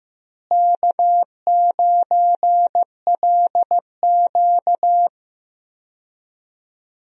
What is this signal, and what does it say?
Morse code, heard as a keyed tone, about 15 wpm, 700 Hz. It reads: K9LQ
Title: USA Call Signs - 4 Characters - Repeated 3x - 15wpm